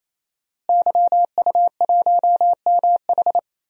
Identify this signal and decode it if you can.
Morse code, heard as a keyed tone, about 27 wpm, 700 Hz. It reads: YU1MH
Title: DXCC-100 Call Signs - 5 Characters - Repeated 3x - 28wpm